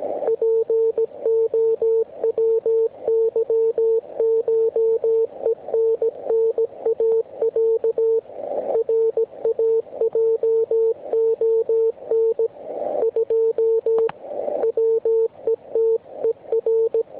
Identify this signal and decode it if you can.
Morse code (Russian alphabet), heard as a keyed tone, about 17 wpm, 450 Hz. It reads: ПОВЫШЕННАЯ РАЙОН 2 ВЕТЕР
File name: ARCHIV / zvuk3 / WX RMP 3748